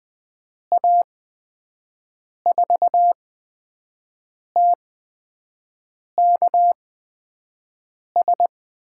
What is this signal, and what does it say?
Morse code, heard as a keyed tone, about 20 wpm, spaced as for 5 wpm, 700 Hz. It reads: A4TKS
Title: USA Call Signs - 6 Characters - 8x Character Spacing - 20wpm